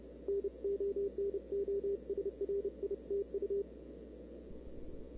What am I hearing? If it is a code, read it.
Morse code, heard as a keyed tone, about 30 wpm, 400 Hz, with radio noise. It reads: NO NO SRI TU
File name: CW / 2005-k5tr arrl cw dx -00:00